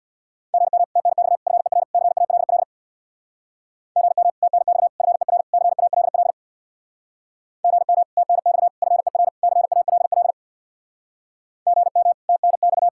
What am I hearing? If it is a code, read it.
Morse code, heard as a keyed tone, about 66 wpm, 700 Hz. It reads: GM TNX FER CALL GM TNX FER CALL GM TNX FER CALL GM TNX